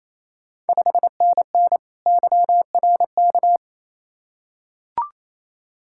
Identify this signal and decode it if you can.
Morse code, heard as a keyed tone, about 28 wpm, 690 Hz. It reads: 5NN YRK
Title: Ontario QSO Party Exchange 28wpm